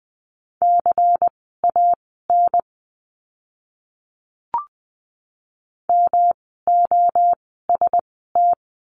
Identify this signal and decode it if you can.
Morse code, heard as a keyed tone, about 20 wpm, spaced as for 15 wpm, 700 Hz. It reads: CAN MOST
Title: 139 Lesson 23f - Letter f - Words 20wpm